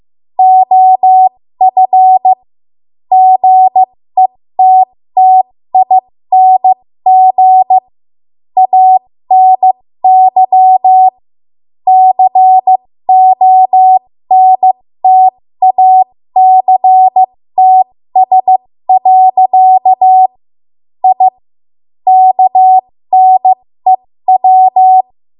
Morse code, heard as a keyed tone, about 15 wpm, spaced as for 13 wpm, 750 Hz. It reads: OF GETTING ANY CONTACTS. I KNEW
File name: Archive / 13 WPM / 13WPM